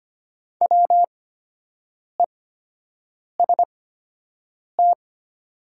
Morse code, heard as a keyed tone, about 25 wpm, spaced as for 7 wpm, 700 Hz. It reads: WEST
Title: 8x Character Spacing - Top 500 Words - 25wpm